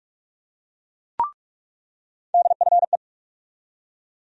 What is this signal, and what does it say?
Morse code, heard as a keyed tone, about 46 wpm, 700 Hz. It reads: DFE
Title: USA Call Sign Postfixes - 3 Letters 45wpm